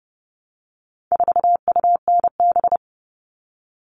Morse code, heard as a keyed tone, about 30 wpm, 700 Hz. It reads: 4UNB
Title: All Cut Numbers - 4-Digit 30wpm